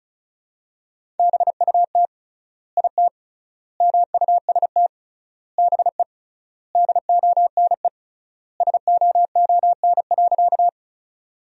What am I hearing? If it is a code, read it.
Morse code, heard as a keyed tone, about 34 wpm, 700 Hz. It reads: BUT IT MUST BE DONE SOON.
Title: 3x Word-Spacing - Sentences from Top 500 Words - 35wpm